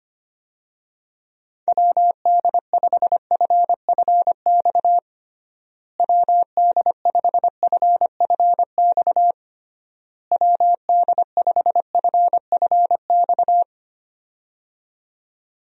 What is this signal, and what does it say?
Morse code, heard as a keyed tone, about 25 wpm, 700 Hz. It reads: WD5FFX WD5FFX WD5FFX
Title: USA Call Signs - 6 Characters - Repeated 3x - 25wpm